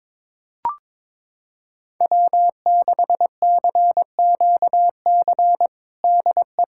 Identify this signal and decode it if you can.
Morse code, heard as a keyed tone, about 22 wpm, 700 Hz. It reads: W6CQC DE
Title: USA Callsign DE Callsign 22wpm